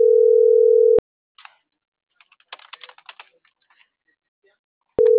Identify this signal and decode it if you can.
Ringback tone (North America), 440 + 480 Hz.